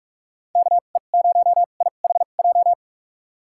Morse code, heard as a keyed tone, about 46 wpm, spaced as for 33 wpm, 700 Hz. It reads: KE0IHJ
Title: USA Call Signs - 6 Characters - 2x Character Spacing - 45wpm